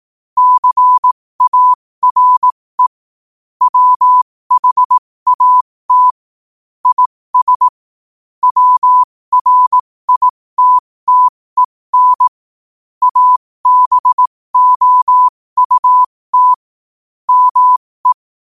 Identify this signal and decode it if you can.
Morse code, heard as a keyed tone, about 18 wpm, spaced as for 15 wpm, 1000 Hz. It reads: CARE WHAT IS WRITTEN ABOUT ME
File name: qotdfiles / 15wpm